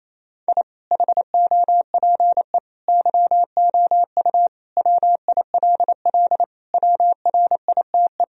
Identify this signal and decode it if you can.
Morse code, heard as a keyed tone, about 28 wpm, 700 Hz. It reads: I HOPE YOU WILL WRITE
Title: Sentences from Top 1000 Words - Review Part 1 - 28wpm